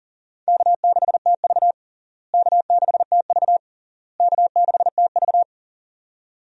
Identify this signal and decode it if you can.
Morse code, heard as a keyed tone, about 40 wpm, 700 Hz. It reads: K6TV K6TV K6TV
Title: USA Call Signs - 4 Characters - Repeated 3x - 40wpm